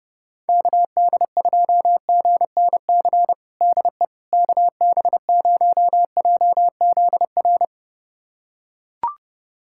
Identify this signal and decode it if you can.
Morse code, heard as a keyed tone, about 30 wpm, 700 Hz. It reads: KD2GNC DE KB0JZR
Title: USA Callsign DE Callsign - 6 Characters - 30wpm